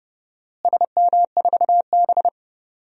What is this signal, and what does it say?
Morse code, heard as a keyed tone, about 29 wpm, 700 Hz. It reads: SM4B
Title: DXCC-100 Call Signs - 4 Characters - Repeated 3x - 30wpm